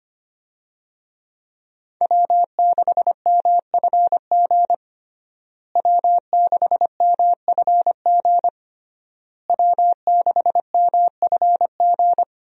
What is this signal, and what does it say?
Morse code, heard as a keyed tone, about 25 wpm, 700 Hz. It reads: W6MFG W6MFG W6MFG
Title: USA Call Signs - 5 Characters - Repeated 3x - 25wpm